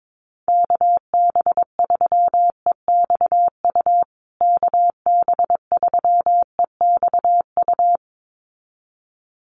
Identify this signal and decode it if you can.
Morse code, heard as a keyed tone, about 22 wpm, 700 Hz. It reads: KB3EXU KB3EXU
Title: USA Call Signs Repeated 2x 22wpm